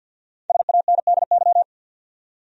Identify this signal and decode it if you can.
Morse code, heard as a keyed tone, about 52 wpm, 700 Hz. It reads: SANDY
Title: USA Common Names - Max 5 Letters - 50wpm